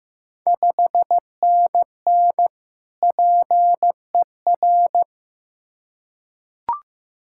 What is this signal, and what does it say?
Morse code, heard as a keyed tone, about 15 wpm, 700 Hz. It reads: 5NN PER
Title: Ontario QSO Party Exchange 15wpm